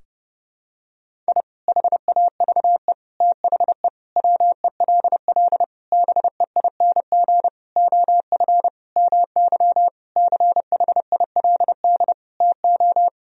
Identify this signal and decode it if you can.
Morse code, heard as a keyed tone, about 30 wpm, 700 Hz. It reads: I HAVE THE WELL BEING OF MY CHILD TO
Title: Sentences from Top 1000 Words - Review Part 3 - 30wpm